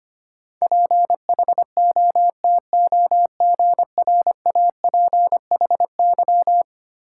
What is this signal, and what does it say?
Morse code, heard as a keyed tone, about 25 wpm, 700 Hz. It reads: PHOTOGRAPHY
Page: morse-code-ninja-practice-25wpm